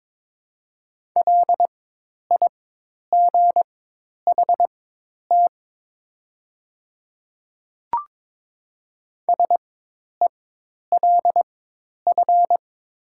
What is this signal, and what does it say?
Morse code, heard as a keyed tone, about 22 wpm, spaced as for 10 wpm, 700 Hz. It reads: LIGHT SELF
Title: 4x Character Spacing - Top 200 Words - 22wpm